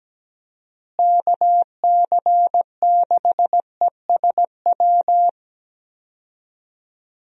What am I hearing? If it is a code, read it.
Morse code, heard as a keyed tone, about 17 wpm, 700 Hz. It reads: KC6ESW
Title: USA Call Signs - 6 Characters - Repeated 2x - 17wpm